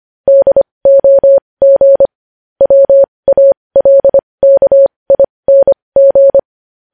Morse code, heard as a keyed tone, about 25 wpm, spaced as for 20 wpm, 550 Hz. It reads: DOG WALKING
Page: news headlines in morse code at 20 wpm